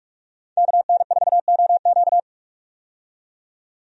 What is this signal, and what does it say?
Morse code, heard as a keyed tone, about 43 wpm, 700 Hz. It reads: KN4OQ
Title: USA Call Signs - 5 Characters - 45wpm